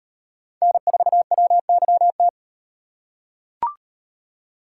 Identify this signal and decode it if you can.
Morse code, heard as a keyed tone, about 38 wpm, 700 Hz. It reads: N4WYT